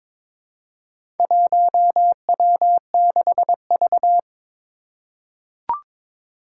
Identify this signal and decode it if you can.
Morse code, heard as a keyed tone, about 22 wpm, 700 Hz. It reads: 1W6V
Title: Four Challenging Letter-Number Combinations 22wpm